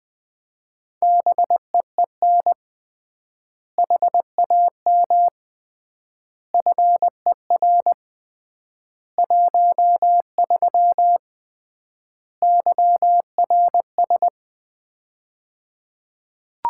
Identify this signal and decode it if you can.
Morse code, heard as a keyed tone, about 20 wpm, 700 Hz. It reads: BEEN HAM FER 13 YRS
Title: Common QSO Phrases - 3x Word-Spacing - 20wpm